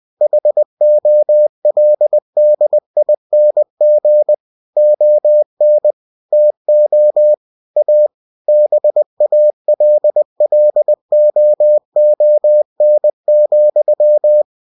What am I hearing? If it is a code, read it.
Morse code, heard as a keyed tone, about 20 wpm, 600 Hz. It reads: HOLDING ON TO A BALLOON,
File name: cw-books / winnie-the-pooh / winnie-20wpm-page-0006